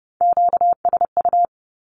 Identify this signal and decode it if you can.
Morse code, heard as a keyed tone, about 30 wpm, 700 Hz. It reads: QSU
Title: Q-Codes with Spoken Meaning - 30wpm